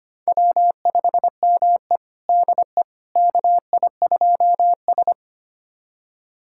Morse code, heard as a keyed tone, about 25 wpm, 700 Hz. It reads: W5ME DE KI2S
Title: USA Callsign DE Callsign - 4 Characters - 25wpm